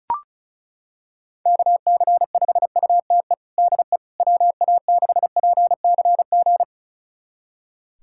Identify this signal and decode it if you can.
Morse code, heard as a keyed tone, about 36 wpm, 700 Hz. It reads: KC5UTE DE WA6PCG
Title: USA Callsign DE Callsign 35wpm